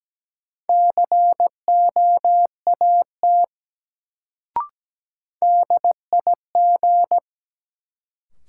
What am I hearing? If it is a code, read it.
Morse code, heard as a keyed tone, about 17 wpm, 700 Hz. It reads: COAT DIG
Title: Sentences from Top 1600 Words - New Word Drill - 17wpm